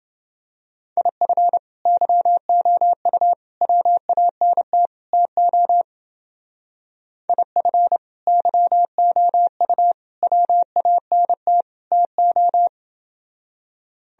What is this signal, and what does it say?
Morse code, heard as a keyed tone, about 30 wpm, 700 Hz. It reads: IF YOU WANT TO IF YOU WANT TO